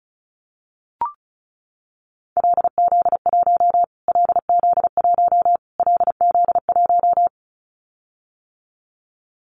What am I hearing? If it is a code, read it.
Morse code, heard as a keyed tone, about 34 wpm, 700 Hz. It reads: LZ1 LZ1 LZ1
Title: Callsign Prefixes for Top 25 DXCC - Repeated 3x - 35wpm